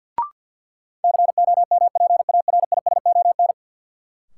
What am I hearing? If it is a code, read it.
Morse code, heard as a keyed tone, about 51 wpm, 700 Hz. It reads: COMPARISON
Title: Sentences from Top 1600 Words - New Word Drill - 50wpm